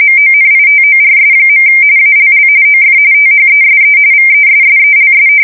RTTY, about 45.45 baud. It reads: MONITORS. ALL OF THESE VESSELS
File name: fsk170 / usnavy3